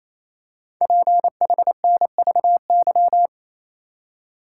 Morse code, heard as a keyed tone, about 28 wpm, 700 Hz. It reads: PHNVY
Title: Code Groups - 5 Letters - 28wpm